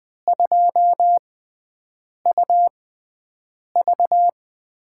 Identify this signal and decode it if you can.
Morse code, heard as a keyed tone, about 20 wpm, spaced as for 7 wpm, 700 Hz. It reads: 2UV
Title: USA Call Signs - 5 Characters - 6x Character Spacing - 20wpm